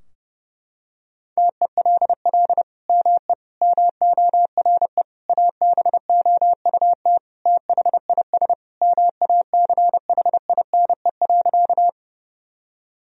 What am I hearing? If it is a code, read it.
Morse code, heard as a keyed tone, about 32 wpm, 700 Hz. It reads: TELL ME MORE ABOUT THIS MACHINE.
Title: Sentences from Top 1000 Words - Review Part 2 - 30wpm